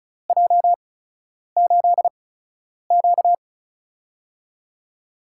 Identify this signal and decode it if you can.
Morse code, heard as a keyed tone, about 36 wpm, spaced as for 10 wpm, 700 Hz. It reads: J8Q
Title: USA Call Signs - 8x Character Spacing - 35wpm